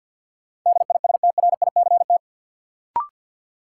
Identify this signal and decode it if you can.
Morse code, heard as a keyed tone, about 53 wpm, 700 Hz. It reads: DISTRICT
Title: Sentences from Top 1600 Words - New Word Drill - 50wpm